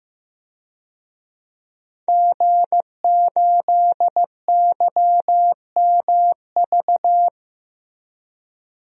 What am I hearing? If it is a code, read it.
Morse code, heard as a keyed tone, about 15 wpm, 700 Hz. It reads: G8YMV
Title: DXCC-100 Call Signs - 5 Characters - Repeated 2x - 15wpm